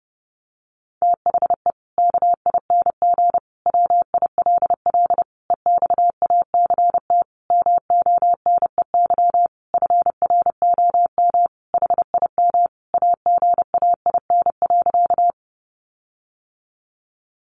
Morse code, heard as a keyed tone, about 30 wpm, 700 Hz. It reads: THE KING WILL EXACT MONEY FROM HIM AGAIN.
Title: Sentences from Top 1300 Words 30wpm